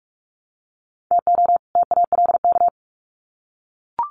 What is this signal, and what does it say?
Morse code, heard as a keyed tone, about 44 wpm, 700 Hz. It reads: TO TALK
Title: Sets of 2 Words 45wpm